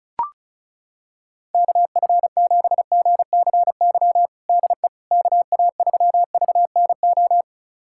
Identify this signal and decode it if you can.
Morse code, heard as a keyed tone, about 36 wpm, 700 Hz. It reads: KF7GCY DE KA3VNO
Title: USA Callsign DE Callsign - 6 Characters - 35wpm